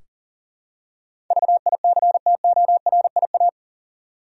Morse code, heard as a keyed tone, about 40 wpm, 700 Hz. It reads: VICTORIA
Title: USA Common Names - 40wpm